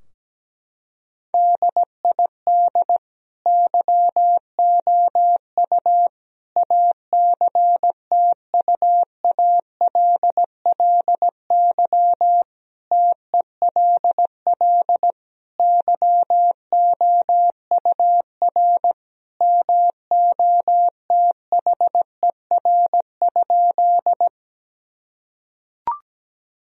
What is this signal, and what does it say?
Morse code, heard as a keyed tone, about 17 wpm, 700 Hz. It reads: DID YOU ACTUALLY TELL YOUR MOTHER?
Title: Sentences from Top 1000 Words - Review Part 4 - 17wpm